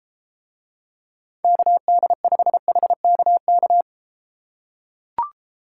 Morse code, heard as a keyed tone, about 32 wpm, 700 Hz. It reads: KD5HKK